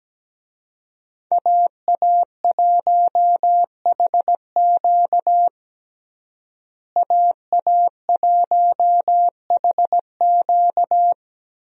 Morse code, heard as a keyed tone, about 17 wpm, 700 Hz. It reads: AA1HQ AA1HQ
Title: USA Call Signs - 5 Characters - Repeated 2x - 17wpm